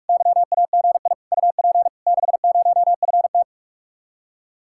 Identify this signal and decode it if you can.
Morse code, heard as a keyed tone, about 52 wpm, 700 Hz. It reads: YAGI UP 60FT
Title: Speed-Racing Common QSO Phrases Encore - 45wpm